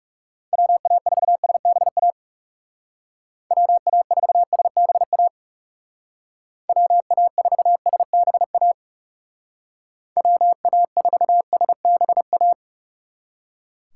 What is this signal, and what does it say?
Morse code, heard as a keyed tone, about 40 wpm, 700 Hz. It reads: WA4SBA WA4SBA WA4SBA WA4SBA